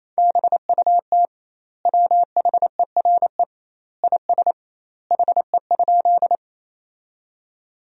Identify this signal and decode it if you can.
Morse code, heard as a keyed tone, about 27 wpm, 700 Hz. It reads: BUT WHERE IS HE?
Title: 2x Word-Spacing - Sentences from Top 200 Words - 28wpm